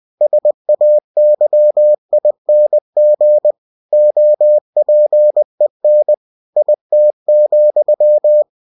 Morse code, heard as a keyed tone, about 20 wpm, 600 Hz. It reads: SAYING OPEN IT,
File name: cw-books / winnie-the-pooh / winnie-20wpm-page-0109